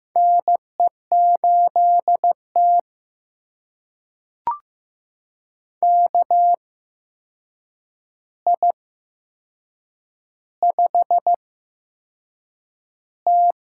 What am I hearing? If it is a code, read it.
Morse code, heard as a keyed tone, about 15 wpm, spaced as for 6 wpm, 700 Hz. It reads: NE8T KI5T
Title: USA Call Signs - 4 Characters - 8x Character Spacing - 15wpm